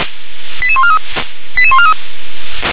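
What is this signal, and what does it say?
Five-tone selective call (ZVEI), heard as quick runs of five tones, tones of about 70 ms. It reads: 88143 88143
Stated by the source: ZVEI five-tone selective call received on an SDR.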